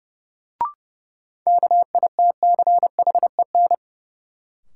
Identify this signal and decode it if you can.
Morse code, heard as a keyed tone, about 30 wpm, 700 Hz. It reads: KITCHEN
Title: Sentences from Top 1400 Words - New Word Drill - 30wpm